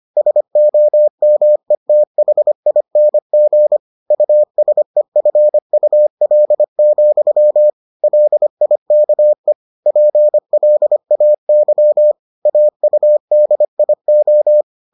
Morse code, heard as a keyed tone, about 25 wpm, 600 Hz. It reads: SOMETHING USEFUL, LIKE PLAY AUDIO